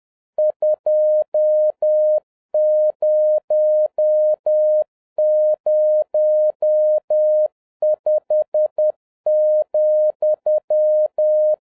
Morse code, heard as a keyed tone, about 10 wpm, 610 Hz. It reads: 2005,